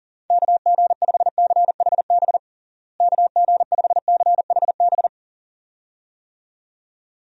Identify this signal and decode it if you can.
Morse code, heard as a keyed tone, about 40 wpm, 700 Hz. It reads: KG5CHB KG5CHB
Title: USA Call Signs - 6 Characters - Repeated 2x - 40wpm